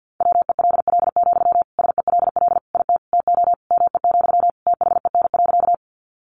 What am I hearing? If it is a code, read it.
Morse code, heard as a keyed tone, about 50 wpm, 700 Hz. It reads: WELL, SELL IT TO ME, THEN.
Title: Sentences from Top 1100 Words 50wpm